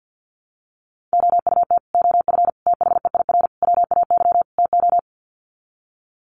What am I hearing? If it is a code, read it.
Morse code, heard as a keyed tone, about 50 wpm, 700 Hz. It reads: OUT OF THEIR WAY TO
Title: Sets of 5 Words Encore2 50wpm